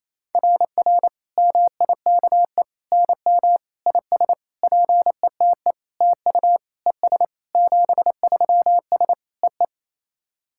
Morse code, heard as a keyed tone, about 28 wpm, 690 Hz. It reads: RR MIKE NM IS PETE TU ES 73S EE
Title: SOTA QSO Practice 28wpm